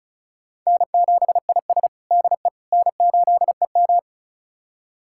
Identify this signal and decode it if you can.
Morse code, heard as a keyed tone, about 36 wpm, 700 Hz. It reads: N7IS DE N8EM
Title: USA Callsign DE Callsign - 4 Characters - 35wpm